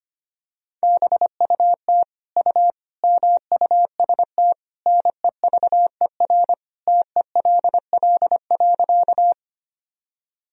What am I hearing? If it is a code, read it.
Morse code, heard as a keyed tone, about 25 wpm, 700 Hz. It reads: BUT U MUST NEVER TELL.
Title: Sentences from Top 300 Words and Abbreviations 25wpm